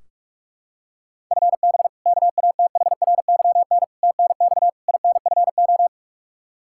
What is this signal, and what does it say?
Morse code, heard as a keyed tone, about 46 wpm, 700 Hz. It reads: FB KATHRYN TNX INFO
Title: Practice QSO - Exchange 3 - More Info - 45wpm